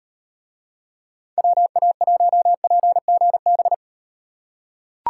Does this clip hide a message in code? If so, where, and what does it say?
Yes: Morse code from 1.4 to 3.7 s, heard as a keyed tone, about 38 wpm, 700 Hz. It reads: WA1PGB